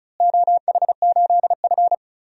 Morse code, heard as a keyed tone, about 36 wpm, 700 Hz. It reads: OH8F
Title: DXCC-100 Call Signs - 4 Characters - Repeated 3x - 35wpm